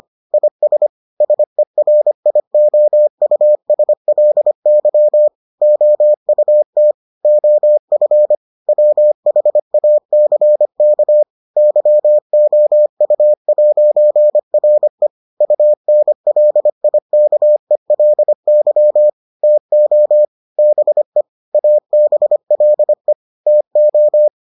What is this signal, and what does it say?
Morse code, heard as a keyed tone, about 25 wpm, 600 Hz. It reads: IS SERIOUSLY OUT OF WHACK YOU'RE UNLIKELY TO BE ABLE TO